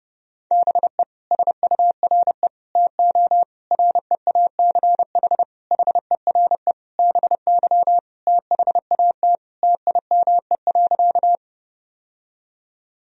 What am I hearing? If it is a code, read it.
Morse code, heard as a keyed tone, about 30 wpm, 700 Hz. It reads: BE SURE TO REACH HERE BY THAT TIME.
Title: Sentences from Top 1000 Words - Review Part 2 - 30wpm